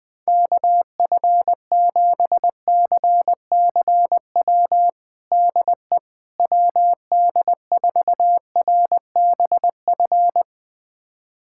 Morse code, heard as a keyed tone, about 19 wpm, 700 Hz. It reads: KF7CCW DE WD4RBF
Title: USA Callsign DE Callsign - 6 Characters - 20wpm